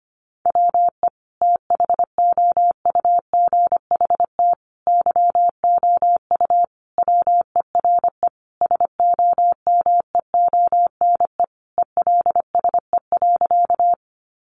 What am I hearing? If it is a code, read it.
Morse code, heard as a keyed tone, about 25 wpm, 700 Hz. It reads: WE THOUGHT YOU WERE SOMEONE ELSE.
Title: Sentences from Top 800 Words 25wpm